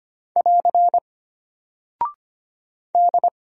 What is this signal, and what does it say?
Morse code, heard as a keyed tone, about 25 wpm, 700 Hz. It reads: <AR> D
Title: Single Character (Letter:Number:Punctuation:Prosign) Alt 25wpm